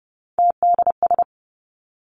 Morse code, heard as a keyed tone, about 30 wpm, 700 Hz. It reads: TDS
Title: Three Challenging Letter-Number Combinations 30wpm